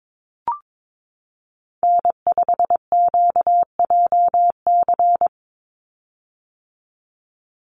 Morse code, heard as a keyed tone, about 22 wpm, 700 Hz. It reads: N5QJC
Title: USA Call Signs 22wpm